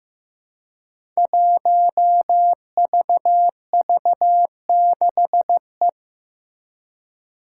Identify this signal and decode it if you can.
Morse code, heard as a keyed tone, about 15 wpm, 700 Hz. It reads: 1VV6E
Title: Five Challenging Letter-Number Combinations 15wpm